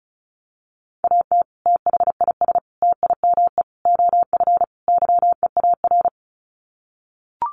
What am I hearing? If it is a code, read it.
Morse code, heard as a keyed tone, about 34 wpm, 700 Hz. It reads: AT THIS TIME OF YEAR
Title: Sets of 5 Words Encore2 35wpm